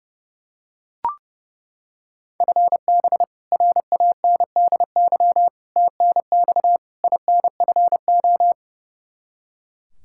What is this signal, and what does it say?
Morse code, heard as a keyed tone, about 30 wpm, 700 Hz. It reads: FB RANDY TNX INFO
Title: Practice QSO - Exchange 3 - More Info - 30wpm